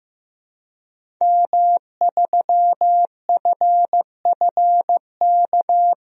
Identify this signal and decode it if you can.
Morse code, heard as a keyed tone, about 15 wpm, 700 Hz. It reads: M3FFK
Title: DXCC-100 Call Signs - 5 Characters - 15wpm